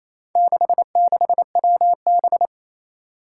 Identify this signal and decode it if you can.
Morse code, heard as a keyed tone, about 28 wpm, 700 Hz. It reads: 66WB
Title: Four Challenging Letter-Number Combinations 28wpm